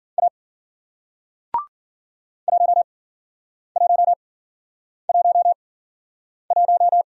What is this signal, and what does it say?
Morse code, heard as a keyed tone, about 52 wpm, spaced as for 9 wpm, 700 Hz. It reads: A 1111